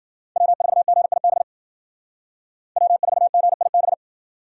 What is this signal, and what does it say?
Morse code, heard as a keyed tone, about 53 wpm, 700 Hz. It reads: W4GIB W4GIB